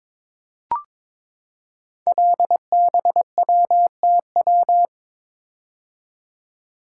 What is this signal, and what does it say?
Morse code, heard as a keyed tone, about 22 wpm, 700 Hz. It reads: LBWTW
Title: Code Groups - 5 Letter:Numbers - 22wpm